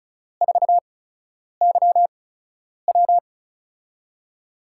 Morse code, heard as a keyed tone, about 36 wpm, spaced as for 10 wpm, 700 Hz. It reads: 4YW
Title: USA Call Signs - 4 Characters - 8x Character Spacing - 35wpm